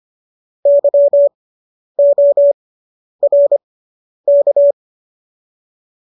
Morse code, heard as a keyed tone, about 25 wpm, spaced as for 10 wpm, 570 Hz. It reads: YORK